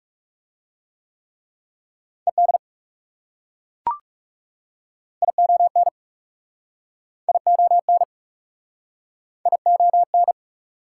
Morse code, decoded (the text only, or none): ED ION ION ION